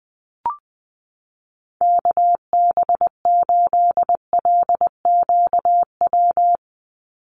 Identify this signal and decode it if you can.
Morse code, heard as a keyed tone, about 20 wpm, 700 Hz. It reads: KB8LQW
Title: USA Call Signs 20wpm